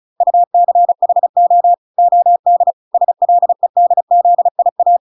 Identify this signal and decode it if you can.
Morse code, heard as a keyed tone, about 34 wpm, 700 Hz. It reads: UCHO OD SLEDZIA